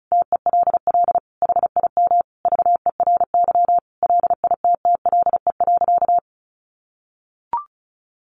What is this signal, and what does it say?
Morse code, heard as a keyed tone, about 34 wpm, 700 Hz. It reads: TELL HIM VERY LITTLE.
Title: Sentences from Top 800 Words 35wpm